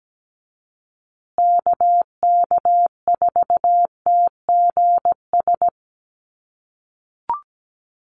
Morse code, heard as a keyed tone, about 17 wpm, 700 Hz. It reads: KK4TGS
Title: USA Call Signs Repeated 3x 17wpm